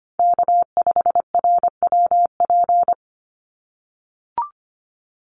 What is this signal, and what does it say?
Morse code, heard as a keyed tone, about 24 wpm, 700 Hz. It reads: K5RWP
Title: NA SOTA Chaser Callsigns 25wpm